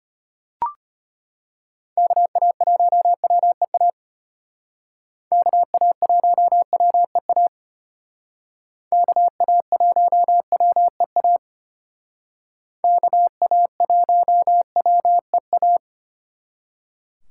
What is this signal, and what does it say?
Morse code, heard as a keyed tone, about 32 wpm, 700 Hz. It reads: KA1WEA KA1WEA KA1WEA KA1WEA